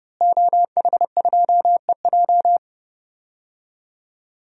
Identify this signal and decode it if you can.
Morse code, heard as a keyed tone, about 30 wpm, 700 Hz. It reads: OH2EJ
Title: DXCC-100 Call Signs - 5 Characters - Repeated 3x - 30wpm